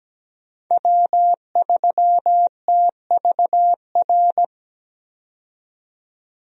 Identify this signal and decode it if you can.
Morse code, heard as a keyed tone, about 17 wpm, 700 Hz. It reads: W3TVR
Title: USA Call Signs - 5 Characters - 17wpm